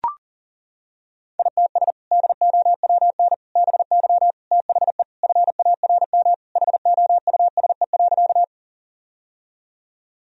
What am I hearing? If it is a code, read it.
Morse code, heard as a keyed tone, about 40 wpm, 700 Hz. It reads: ITS DOWN BY THE FARM HOUSE.
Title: Sentences from Top 1000 Words - Review Part 2 - 40wpm